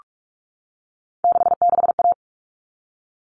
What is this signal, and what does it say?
Morse code, heard as a keyed tone, about 44 wpm, 700 Hz. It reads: 66A
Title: All Cut Numbers - 3-Digit 45wpm